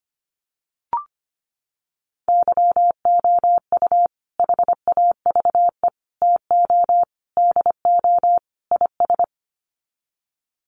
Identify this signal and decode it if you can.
Morse code, heard as a keyed tone, about 25 wpm, 700 Hz. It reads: YOU HAVE TO DO IS
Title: Sets of 5 Words 25wpm